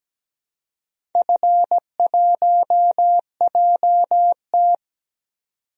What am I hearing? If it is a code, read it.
Morse code, heard as a keyed tone, about 17 wpm, 700 Hz. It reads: F1JT
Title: DXCC-100 Call Signs - 4 Characters - 17wpm